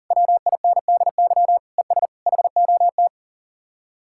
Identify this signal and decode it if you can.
Morse code, heard as a keyed tone, about 52 wpm, 700 Hz. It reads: WINDY ES HOT